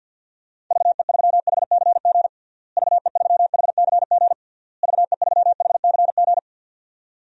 Morse code, heard as a keyed tone, about 47 wpm, 700 Hz. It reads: VE3HCG VE3HCG VE3HCG
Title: DXCC-100 Call Signs - 6 Characters - Repeated 3x - 50wpm